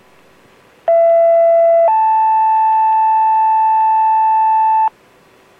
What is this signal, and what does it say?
Two-tone page, heard as a one-second tone, then a three-second tone, about 670 then 880 Hz, with noise. It is heard from 0.9 to 4.9 s.